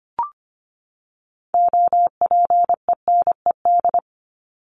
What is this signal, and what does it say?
Morse code, heard as a keyed tone, about 25 wpm, 700 Hz. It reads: OPENED
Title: ED Suffix - Top 100 Words with Base Words 25wpm